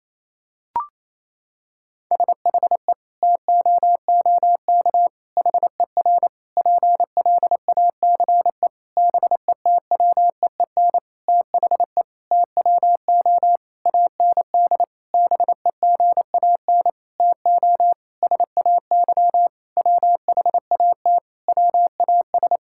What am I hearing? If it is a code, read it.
Morse code, heard as a keyed tone, about 28 wpm, 700 Hz. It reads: SHE TOOK HER PLACE BETWEEN THE TWO AND BEGAN TO SAY WHAT WAS
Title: Sentences from Top 1000 Words - Review Part 5 - 28wpm